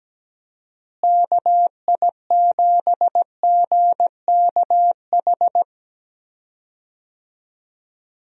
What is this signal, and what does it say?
Morse code, heard as a keyed tone, about 17 wpm, 700 Hz. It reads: KI7GKH
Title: USA Call Signs - 6 Characters - Repeated 2x - 17wpm